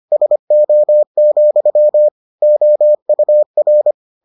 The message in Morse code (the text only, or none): SO, OUR